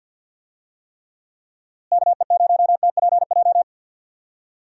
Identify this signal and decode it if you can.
Morse code, heard as a keyed tone, about 51 wpm, 700 Hz. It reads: KE0TPJ
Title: USA Call Signs - 6 Characters - Repeated 2x - 50wpm